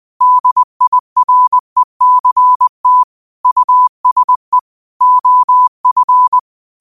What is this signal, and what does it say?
Morse code, heard as a keyed tone, about 20 wpm, 1000 Hz. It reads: DIRECT USE OF
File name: qotdfiles / 20wpm